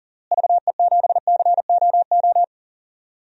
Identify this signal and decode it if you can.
Morse code, heard as a keyed tone, about 40 wpm, 700 Hz. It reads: VE7COO
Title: DXCC-100 Call Signs - 6 Characters - Repeated 3x - 40wpm